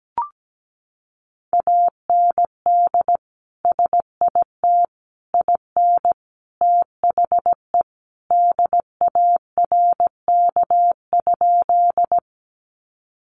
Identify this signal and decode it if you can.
Morse code, heard as a keyed tone, about 17 wpm, 700 Hz. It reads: AND SIT IN THE DARK?
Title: Sentences from Top 500 Words 17wpm